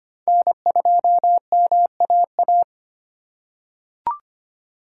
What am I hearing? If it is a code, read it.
Morse code, heard as a keyed tone, about 25 wpm, 700 Hz. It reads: N2MAA
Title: USA Call Signs - 5 Characters - 2x Character Spacing - 25wpm